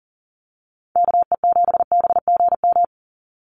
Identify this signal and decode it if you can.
Morse code, heard as a keyed tone, about 40 wpm, 700 Hz. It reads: KE7BGM
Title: NA SOTA Activator Callsigns 40wpm